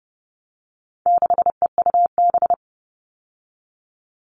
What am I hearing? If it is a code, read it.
Morse code, heard as a keyed tone, about 28 wpm, 700 Hz. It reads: 6EUB
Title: All Cut Numbers - 4-Digit 30wpm